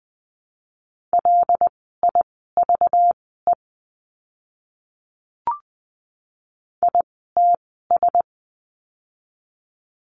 Morse code, heard as a keyed tone, about 20 wpm, spaced as for 14 wpm, 700 Hz. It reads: LIVE ITS
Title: 237 Lesson 38f - Letter q - Words 20wpm